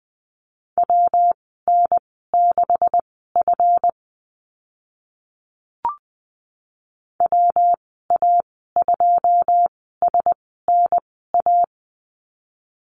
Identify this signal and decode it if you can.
Morse code, heard as a keyed tone, about 20 wpm, spaced as for 14 wpm, 700 Hz. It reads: WN6F WA2SNA
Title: 161 Lesson 26g - Letter g - Call Signs 20wpm